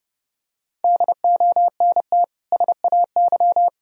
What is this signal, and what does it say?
Morse code, heard as a keyed tone, about 30 wpm, 700 Hz. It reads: DONT SAY
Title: Sentences from Top 1000 Words - Review Part 1 - 30wpm